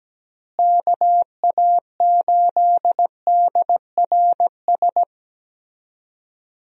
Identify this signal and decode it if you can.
Morse code, heard as a keyed tone, about 17 wpm, 700 Hz. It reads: KA8DRS
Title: USA Call Signs - 6 Characters - Repeated 2x - 17wpm